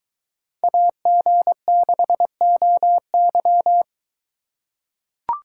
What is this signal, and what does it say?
Morse code, heard as a keyed tone, about 23 wpm, 700 Hz. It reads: AG6OY